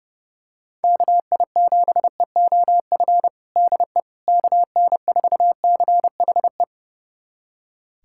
Morse code, heard as a keyed tone, about 30 wpm, 700 Hz. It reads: KI7EOF DE KN4CHE
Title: USA Callsign DE Callsign - 6 Characters - 30wpm